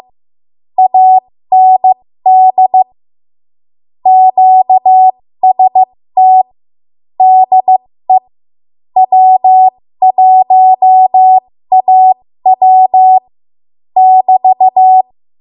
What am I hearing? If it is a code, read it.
Morse code, heard as a keyed tone, about 15 wpm, spaced as for 13 wpm, 750 Hz. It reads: AND QST DE W1AW <BT>
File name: Morse / 13WPM